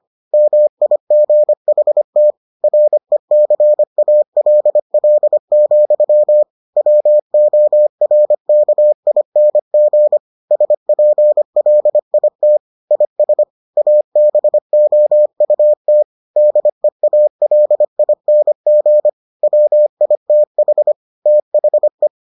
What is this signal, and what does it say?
Morse code, heard as a keyed tone, about 25 wpm, 600 Hz. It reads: MIGHT RECALL, WORKING SPLIT IS ABOUT DEALING WITH THE